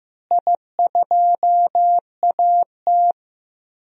Morse code, heard as a keyed tone, about 15 wpm, 700 Hz. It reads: I2AT
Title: DXCC-100 Call Signs - 4 Characters - Repeated 2x - 15wpm